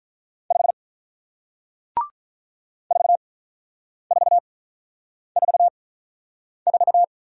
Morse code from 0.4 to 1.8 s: 5

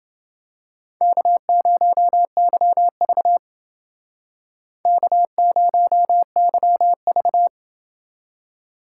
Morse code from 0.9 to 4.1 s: K0YV

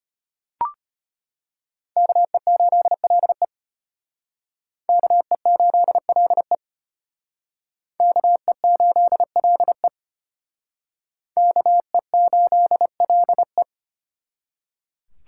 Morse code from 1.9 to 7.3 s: KE8LE KE8LE

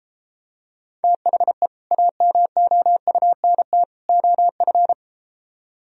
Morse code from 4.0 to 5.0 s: OF

